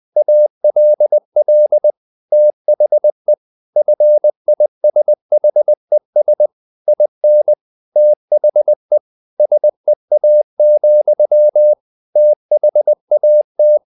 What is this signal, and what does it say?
Morse code, heard as a keyed tone, about 20 wpm, 600 Hz. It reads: ALL THE FISHES IN THE SEA, THAT
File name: the old man and the sea part 2 20wpm